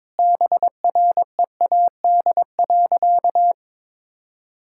Morse code, heard as a keyed tone, about 22 wpm, 700 Hz. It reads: BREAD.
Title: Sentences from Top 1600 Words 22wpm